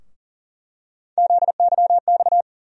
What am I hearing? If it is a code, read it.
Morse code, heard as a keyed tone, about 40 wpm, 700 Hz. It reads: ZYX
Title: USA Call Sign Postfixes - 3 Letters 40wpm